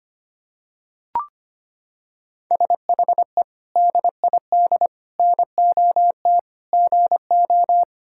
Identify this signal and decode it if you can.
Morse code, heard as a keyed tone, about 25 wpm, 700 Hz. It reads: SHE DID NOT GO
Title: Sentences from Top 1000 Words - Review Part 6 - 25wpm